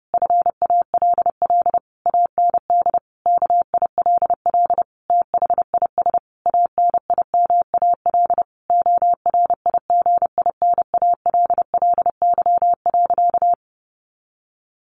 Morse code, heard as a keyed tone, about 30 wpm, 700 Hz. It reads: FALL AND KILL THIS ANIMAL ORIGINALLY.
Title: A Sound Of Thunder 30wpm